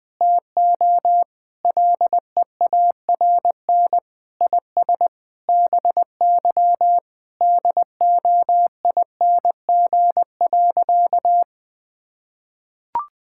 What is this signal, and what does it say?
Morse code, heard as a keyed tone, about 20 wpm, 700 Hz. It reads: TO LEARN IS BY DOING.
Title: Sentences with 6-Word Phrases - 20wpm